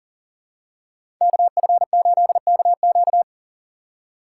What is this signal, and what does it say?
Morse code, heard as a keyed tone, about 40 wpm, 700 Hz. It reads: KF8KQ
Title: USA Call Signs - 5 Characters - 6x Character Spacing - 40wpm